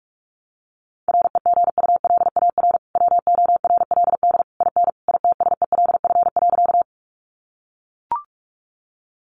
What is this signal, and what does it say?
Morse code, heard as a keyed tone, about 44 wpm, 700 Hz. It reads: REGULAR WORLD IN ITSELF.
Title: Sentences from Top 900 Words 45wpm